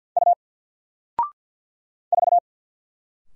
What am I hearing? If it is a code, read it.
Morse code, heard as a keyed tone, about 48 wpm, 700 Hz. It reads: U 4
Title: Single Letter-Number 50wpm